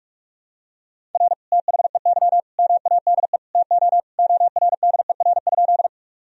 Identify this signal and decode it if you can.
Morse code, heard as a keyed tone, about 47 wpm, 700 Hz. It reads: R THEY MADE TO ORDER?
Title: Sentences from Top 400 Words and Abbreviations 45wpm